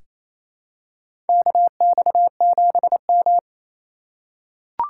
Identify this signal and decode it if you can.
Morse code, heard as a keyed tone, about 28 wpm, 700 Hz. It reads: KX7M
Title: K1USN SST QSO Practice - 28wpm